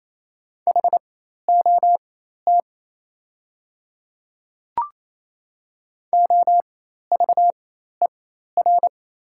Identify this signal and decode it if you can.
Morse code, heard as a keyed tone, about 28 wpm, spaced as for 13 wpm, 690 Hz. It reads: HOT OVER
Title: 4x Character Spacing - Top 100 Words - 28wpm